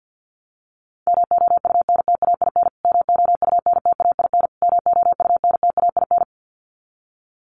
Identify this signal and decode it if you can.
Morse code, heard as a keyed tone, about 50 wpm, 700 Hz. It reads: MOUNTAIN MOUNTAIN MOUNTAIN
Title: Top 500 Words Repeated 50wpm